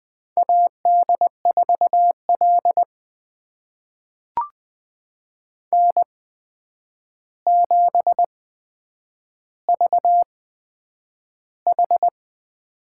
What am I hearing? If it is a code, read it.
Morse code, heard as a keyed tone, about 20 wpm, spaced as for 9 wpm, 700 Hz. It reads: AD4L N7VH
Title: USA Call Signs - 4 Characters - 8x Character Spacing - 20wpm